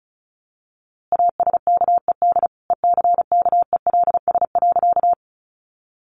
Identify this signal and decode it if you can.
Morse code, heard as a keyed tone, about 34 wpm, 700 Hz. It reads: ASKED ECKELS.
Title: A Sound Of Thunder 35wpm